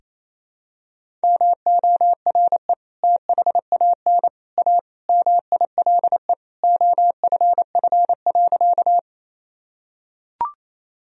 Morse code, heard as a keyed tone, about 28 wpm, 700 Hz. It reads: MORE THAN A MILE OFF.
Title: Sentences from Top 1000 Words - Review Part 1 - 28wpm